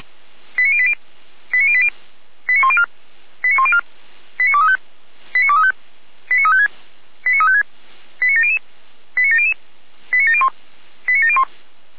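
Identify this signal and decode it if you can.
Five-tone selective call (ZVEI), heard as quick runs of five tones, tones of about 80 ms. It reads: 89080 89080 89104 89104 89235 89235 89356 89356 89700 89700 89701 89701
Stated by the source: ZVEI five-tone call received on an SDR.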